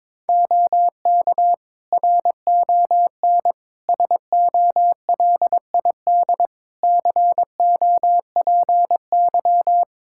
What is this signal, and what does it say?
Morse code, heard as a keyed tone, about 22 wpm, 700 Hz. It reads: OK RON SOLID COPY
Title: Practice QSO - Exchange 3 - More Info - 22wpm